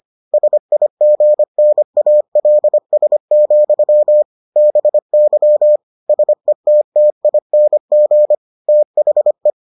Morse code, heard as a keyed tone, about 25 wpm, 600 Hz. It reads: SIGNALS, BY SETTING THE